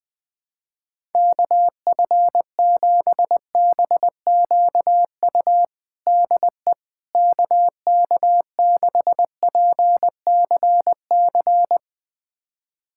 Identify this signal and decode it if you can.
Morse code, heard as a keyed tone, about 20 wpm, 700 Hz. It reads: KF7BQU DE KK6PCC
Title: USA Callsign DE Callsign 20wpm